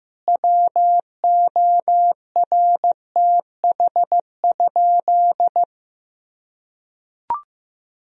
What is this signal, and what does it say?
Morse code, heard as a keyed tone, about 16 wpm, 700 Hz. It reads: WORTH?
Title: Sentences from Top 1000 Words - Review Part 2 - 15wpm